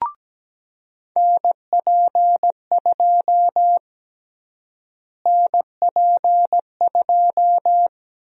Morse code, heard as a keyed tone, about 17 wpm, 700 Hz. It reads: NP2 NP2
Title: USA Call Sign Prefixes - Repeated 2x - 17wpm